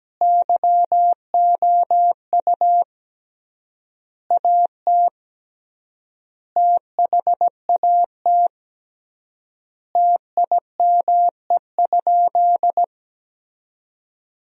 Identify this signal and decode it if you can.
Morse code, heard as a keyed tone, about 17 wpm, 700 Hz. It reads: YOU AT THAT TIME?
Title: 3x Word-Spacing - Sentences from Top 200 Words - 17wpm